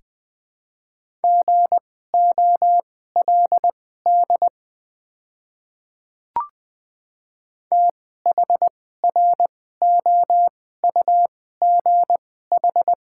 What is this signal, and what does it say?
Morse code, heard as a keyed tone, about 20 wpm, spaced as for 15 wpm, 700 Hz. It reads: GOLD THROUGH
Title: 2x Character Spacing - Top 500 Words - 20wpm